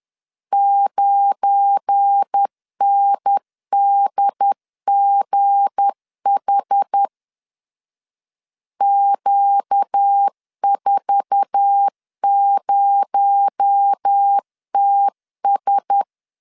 Morse code, heard as a keyed tone, about 10 wpm, 800 Hz. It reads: 9NDGH Q40TS
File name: LEZIONI / 8 YJ45 / 8 YJ4 10wpm 2633